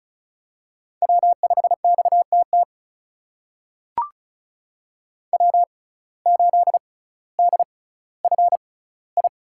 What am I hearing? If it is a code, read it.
Morse code, heard as a keyed tone, about 36 wpm, spaced as for 18 wpm, 700 Hz. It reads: W5XTT W8DFI
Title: USA Call Signs - 5 Characters - 6x Character Spacing - 35wpm